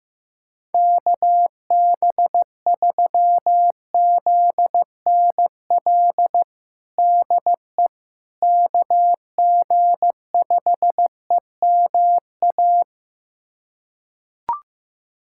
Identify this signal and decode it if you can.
Morse code, heard as a keyed tone, about 15 wpm, 700 Hz. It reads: KB3ZNL DE KG5EMA
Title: USA Callsign DE Callsign - 6 Characters - 15wpm